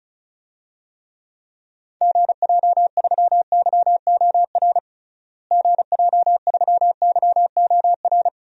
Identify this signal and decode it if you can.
Morse code, heard as a keyed tone, about 36 wpm, 700 Hz. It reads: GJ3YOR GJ3YOR
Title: DXCC-100 Call Signs - 6 Characters - Repeated 3x - 35wpm